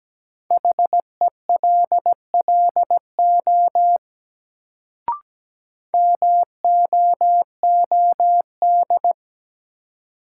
Morse code, heard as a keyed tone, about 17 wpm, 700 Hz. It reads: HELLO MOOD
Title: Sentences from Top 1100 Words - New Word Drill - 17wpm